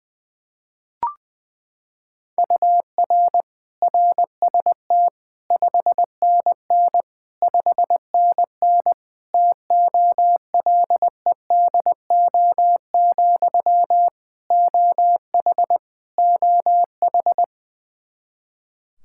Morse code, heard as a keyed tone, about 20 wpm, 700 Hz. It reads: UR RST 5NN 5NN TOLEDO, OH OH
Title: Practice QSO - Exchange 1 - Beginning - 20wpm